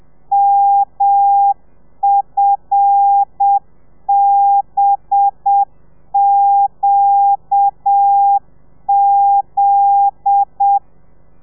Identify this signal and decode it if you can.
Morse code, heard as a keyed tone, about 7 wpm, 800 Hz. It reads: MFBQZ